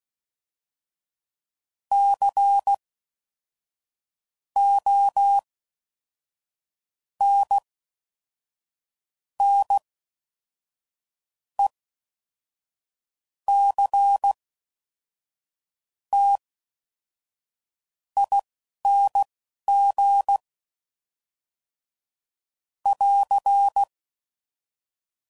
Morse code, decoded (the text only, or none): CONNECTING <AR>